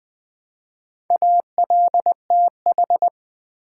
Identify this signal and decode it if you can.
Morse code, heard as a keyed tone, about 20 wpm, 700 Hz. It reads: ALTH
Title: Build A Word - 2 Letters At A Time - 8-Letter Words - 20wpm